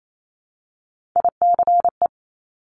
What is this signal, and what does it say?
Morse code, heard as a keyed tone, about 28 wpm, 700 Hz. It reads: ICE
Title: Multi-letter Phonemes Repeated 3x 28wpm